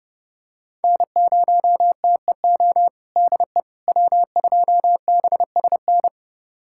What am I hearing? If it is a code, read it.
Morse code, heard as a keyed tone, about 30 wpm, 700 Hz. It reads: N0TEO DE W2BSN
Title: USA Callsign DE Callsign - 5 Characters - 30wpm